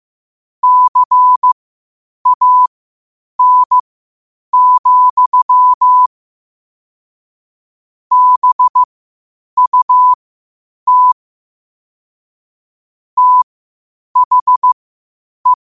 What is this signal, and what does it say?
Morse code, heard as a keyed tone, about 15 wpm, spaced as for 8 wpm, 1000 Hz. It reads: CAN, BUT THE
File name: qotdfiles / 8wpm